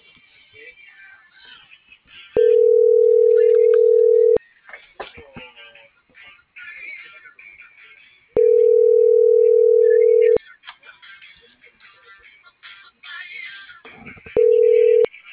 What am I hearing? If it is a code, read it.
Ringback tone (North America), 440 + 480 Hz.